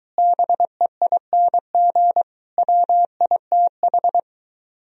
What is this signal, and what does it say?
Morse code, heard as a keyed tone, about 23 wpm, 700 Hz. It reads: BEING WITH